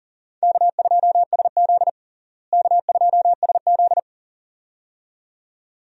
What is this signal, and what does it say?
Morse code, heard as a keyed tone, about 40 wpm, 700 Hz. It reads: K2SZ K2SZ
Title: USA Call Signs - 4 Characters - Repeated 2x - 40wpm